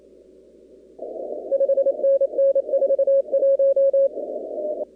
Morse code, heard as a keyed tone, about 28 wpm, 1650 Hz, with noise. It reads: 5NN 41
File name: contests / 2019-cw-wpx-kt5j-k5pi- -2330 0-44